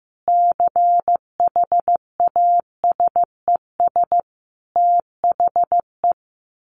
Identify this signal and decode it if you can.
Morse code, heard as a keyed tone, about 15 wpm, 700 Hz. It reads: CHASES THE
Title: Additional Easy Phrases 15wpm